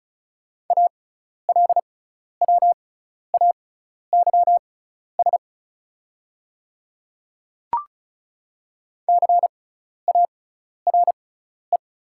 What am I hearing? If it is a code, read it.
Morse code, heard as a keyed tone, about 36 wpm, spaced as for 12 wpm, 690 Hz. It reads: ALWAYS CARE
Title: 6x Character Spacing - Top 300 Words - 35wpm